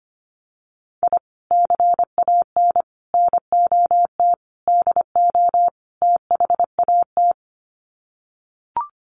Morse code, heard as a keyed tone, about 25 wpm, 700 Hz. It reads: I CAN NOT DO THAT
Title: Sets of 5 Words Encore2 25wpm